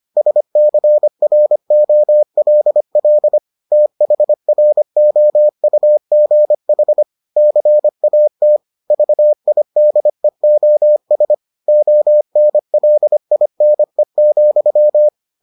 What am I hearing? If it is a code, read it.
Morse code, heard as a keyed tone, about 25 wpm, 600 Hz. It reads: SCROLL THROUGH CAT VIDEOS ONLINE,